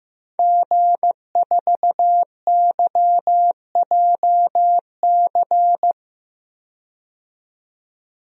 Morse code, heard as a keyed tone, about 15 wpm, 700 Hz. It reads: G4YJC
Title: DXCC-100 Call Signs - 5 Characters - 15wpm